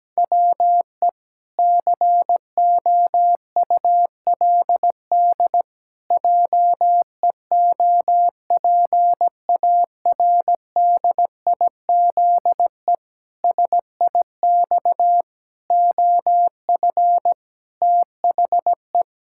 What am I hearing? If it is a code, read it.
Morse code, heard as a keyed tone, about 17 wpm, 700 Hz. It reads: WE COULD JEOPARDIZE SIX OF THE
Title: Pangram Challenge v2 - 17wpm